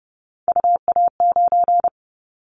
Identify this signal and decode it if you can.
Morse code, heard as a keyed tone, about 30 wpm, 700 Hz. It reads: UA9
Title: Callsign Prefixes for Top 50 DXCC - 30wpm